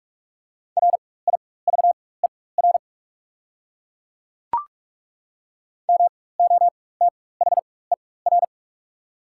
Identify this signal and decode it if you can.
Morse code, heard as a keyed tone, about 43 wpm, spaced as for 21 wpm, 690 Hz. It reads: RIVER MOTHER
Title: 4x Character Spacing - Top 300 Words - 45wpm